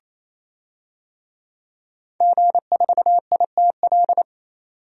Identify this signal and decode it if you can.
Morse code, heard as a keyed tone, about 28 wpm, 700 Hz. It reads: G4ITL
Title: DXCC-100 Call Signs - 5 Characters - 28wpm